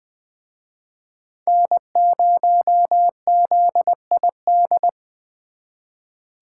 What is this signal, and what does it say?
Morse code, heard as a keyed tone, about 24 wpm, 700 Hz. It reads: N0ZID
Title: Speed Racing - USA 1x3 Call Signs 20wpm